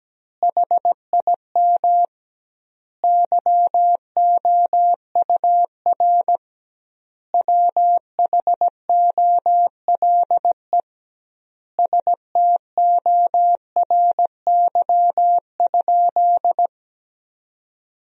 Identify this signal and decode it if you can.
Morse code, heard as a keyed tone, about 17 wpm, 700 Hz. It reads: HIM YOUR WHOLE STORY?
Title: 2x Word-Spacing - Sentences from Top 400 Words - 17wpm